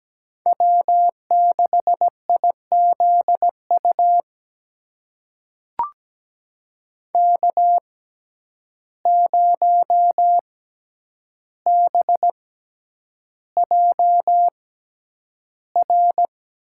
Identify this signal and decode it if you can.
Morse code, heard as a keyed tone, about 17 wpm, spaced as for 9 wpm, 700 Hz. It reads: W6IZU K0BJR
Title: USA Call Signs - 6x Character Spacing - 17wpm